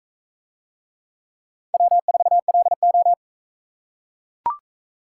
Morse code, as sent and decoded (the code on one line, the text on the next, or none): .-- ....- .-.. ---
W4LO